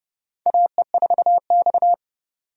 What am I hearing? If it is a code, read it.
Morse code, heard as a keyed tone, about 30 wpm, 700 Hz. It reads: AE4X
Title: USA Call Signs - 4 Characters - 30wpm